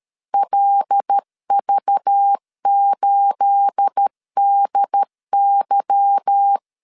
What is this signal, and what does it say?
Morse code, heard as a keyed tone, about 13 wpm, 800 Hz. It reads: LV8DY
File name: LEZIONI / 9 28 12wpm 2351